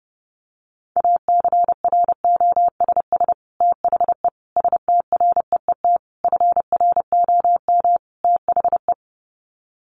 Morse code, heard as a keyed tone, about 30 wpm, 700 Hz. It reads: ACROSS THE STREET FROM THE
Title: Sets of 5 Words Encore 30wpm